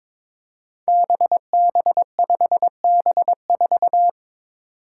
Morse code, heard as a keyed tone, about 21 wpm, 700 Hz. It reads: BB5B4
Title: Five Challenging Letter-Number Combinations 22wpm